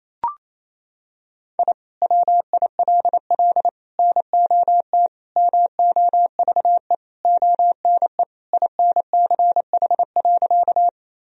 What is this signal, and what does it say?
Morse code, heard as a keyed tone, about 28 wpm, 700 Hz. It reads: I WILL NOT MOVE ONE INCH.
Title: Sentences from Top 1000 Words - Review Part 3 - 28wpm